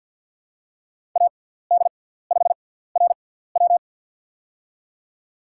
Morse code, heard as a keyed tone, about 52 wpm, spaced as for 17 wpm, 700 Hz. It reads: AD5RW
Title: USA Call Signs - 5 Characters - 6x Character Spacing - 50wpm